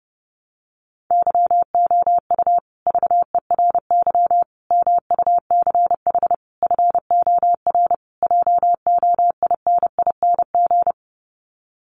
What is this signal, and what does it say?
Morse code, heard as a keyed tone, about 30 wpm, 700 Hz. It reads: YOU VERY MUCH FOR JOINING
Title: Sets of 5 Words Encore 30wpm